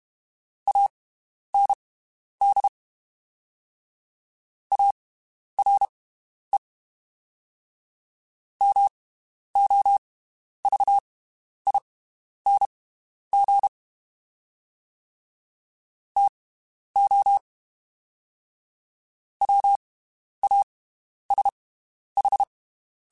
Morse code, heard as a keyed tone, about 32 wpm, spaced as for 11 wpm, 800 Hz. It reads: AND ARE MOVING TO WASH
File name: (spaced) / QSOs Spaced-mp3s / WD0BGP 1 @30 sp